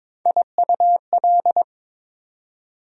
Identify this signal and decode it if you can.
Morse code, heard as a keyed tone, about 22 wpm, 700 Hz. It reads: IUL
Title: USA Call Sign Postfixes - 3 Letters 22wpm